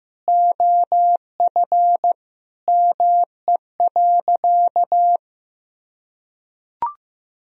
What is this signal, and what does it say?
Morse code, heard as a keyed tone, about 15 wpm, 700 Hz. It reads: OF ME.
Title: Sentences from Top 1000 Words - Review Part 1 - 15wpm